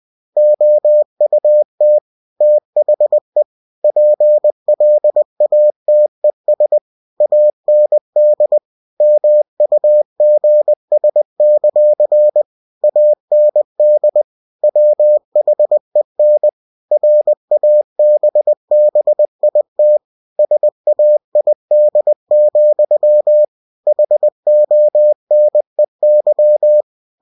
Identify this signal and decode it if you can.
Morse code, heard as a keyed tone, about 20 wpm, 600 Hz. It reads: OUT THE PLATES AND MUGS; AND WHEN RABBIT SAID, HONEY
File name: cw-books / winnie-the-pooh / winnie-20wpm-page-0015